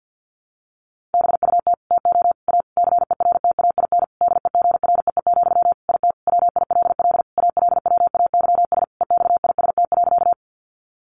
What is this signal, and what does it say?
Morse code, heard as a keyed tone, about 50 wpm, 700 Hz. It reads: BUT TO A CERTAIN DEGREE, IT WILL ALWAYS EXIST.
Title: Sentences from Top 1500 Words 50wpm